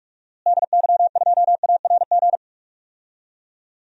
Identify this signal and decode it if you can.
Morse code, heard as a keyed tone, about 46 wpm, 700 Hz. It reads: DY2ARG
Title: DXCC-100 Call Signs - 6 Characters - Repeated 3x - 45wpm